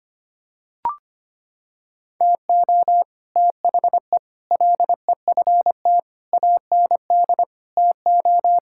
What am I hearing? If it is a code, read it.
Morse code, heard as a keyed tone, about 25 wpm, 700 Hz. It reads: TO THE LEFT AND TO
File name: practice / 25wpm / Sentences with 9-Word Phrases - 25wpm